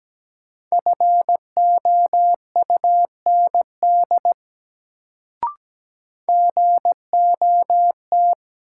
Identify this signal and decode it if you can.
Morse code, heard as a keyed tone, about 17 wpm, 700 Hz. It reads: FOUND GOT
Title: Sentences from Top 300 Words - New Word Drill - 17wpm